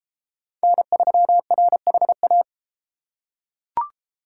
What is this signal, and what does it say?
Morse code, heard as a keyed tone, about 32 wpm, 700 Hz. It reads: N3RHA